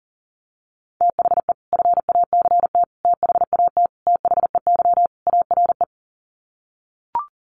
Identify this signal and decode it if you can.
Morse code, heard as a keyed tone, about 37 wpm, 700 Hz. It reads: THE FACT THAT THEY ARE
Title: Sets of 5 Words Encore2 40wpm